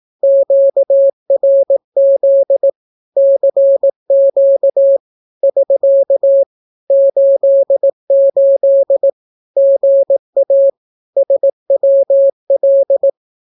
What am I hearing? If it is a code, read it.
Morse code, heard as a keyed tone, about 18 wpm, 550 Hz. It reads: QRZ CQ <SK> 88 GA SWL
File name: naukapodstaw / skroty / skroty